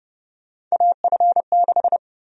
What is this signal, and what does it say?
Morse code, heard as a keyed tone, about 30 wpm, 700 Hz. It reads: AF6
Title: USA Call Sign Prefixes - 30wpm